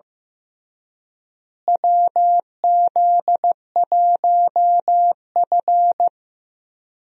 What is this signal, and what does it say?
Morse code, heard as a keyed tone, about 15 wpm, 700 Hz. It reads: WZ1F
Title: USA Call Signs - 4 Characters - Repeated 3x - 15wpm